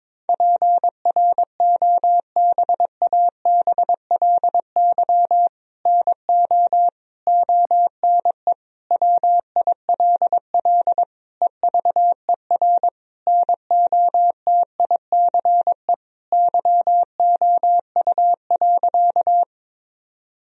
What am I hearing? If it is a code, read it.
Morse code, heard as a keyed tone, about 21 wpm, 700 Hz. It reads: PROBABLY NO ONE WILL EVER NOTICE YOU.
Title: Sentences from Top 1000 Words - Review Part 4 - 22wpm